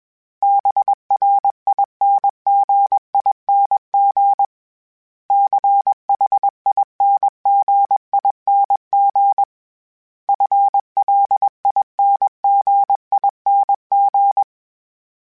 Morse code, heard as a keyed tone, about 21 wpm, 800 Hz. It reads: BRINGING CHINGING FLINGING